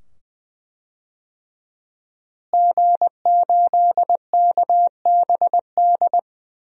Morse code, heard as a keyed tone, about 20 wpm, 700 Hz. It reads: G8KBD
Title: DXCC-100 Call Signs - 5 Characters - Repeated 2x - 20wpm